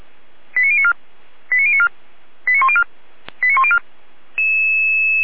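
Five-tone selective call (ZVEI), heard as quick runs of five tones, tones of about 70 ms. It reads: 89094 89094 89104 89104
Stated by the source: Five-tone ZVEI selective call 89276, received on SDR, recorded by feuerwehr-orlamuende (fire brigade Orlamünde).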